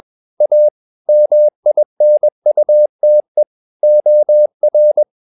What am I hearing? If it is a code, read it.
Morse code, heard as a keyed tone, about 21 wpm, 600 Hz. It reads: A MINUTE OR
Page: CW med Gnister